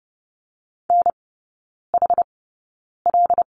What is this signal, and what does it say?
Morse code, heard as a keyed tone, about 30 wpm, spaced as for 9 wpm, 700 Hz. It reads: NHL
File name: Lesson 18e - Letter m - Three Characters 30wpm